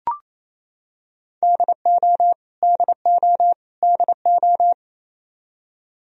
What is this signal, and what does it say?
Morse code, heard as a keyed tone, about 28 wpm, 700 Hz. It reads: DO DO DO
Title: Top 100 Words Repeated 28wpm